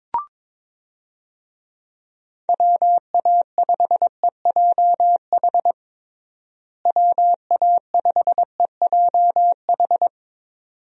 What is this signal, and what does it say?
Morse code, heard as a keyed tone, about 22 wpm, 700 Hz. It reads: WA5EJH WA5EJH
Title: USA Call Signs - 6 Characters - Repeated 2x - 22wpm